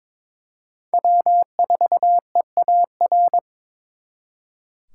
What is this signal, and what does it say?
Morse code, heard as a keyed tone, about 26 wpm, 700 Hz. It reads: W4EAR
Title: Speed Racing - USA 1x3 Call Signs 22wpm